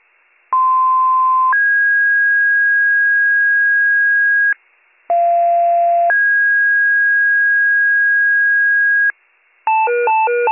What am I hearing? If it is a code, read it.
Two-tone page, heard as a one-second tone, then a three-second tone, about 1040 then 1650 Hz, about 690 then 1650 Hz. It is heard from 0.5 to 9.1 s.